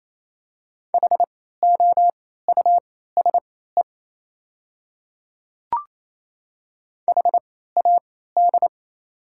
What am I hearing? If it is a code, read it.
Morse code, heard as a keyed tone, about 28 wpm, spaced as for 16 wpm, 700 Hz. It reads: HOUSE HAD
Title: 3x Character Spacing - Top 300 Words - 28wpm